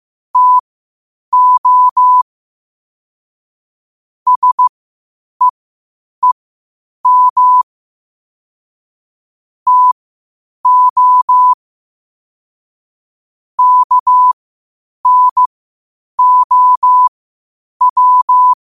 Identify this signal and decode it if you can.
Morse code, heard as a keyed tone, about 15 wpm, spaced as for 8 wpm, 1000 Hz. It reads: TO SEEM TO KNOW
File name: qotdfiles / 8wpm